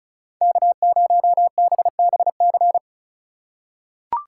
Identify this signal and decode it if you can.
Morse code, heard as a keyed tone, about 36 wpm, 700 Hz. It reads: K0BBC
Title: USA Call Signs - 3x Character Spacing - 35wpm